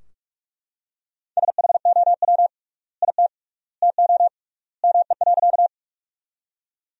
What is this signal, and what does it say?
Morse code, heard as a keyed tone, about 41 wpm, 700 Hz. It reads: SHOW IT TO ME.
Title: 3x Word-Spacing - Sentences from Top 200 Words - 45wpm